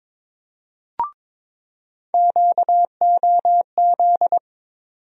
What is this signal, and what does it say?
Morse code, heard as a keyed tone, about 22 wpm, 700 Hz. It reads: QOZ
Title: USA Call Sign Postfixes - 3 Letters 22wpm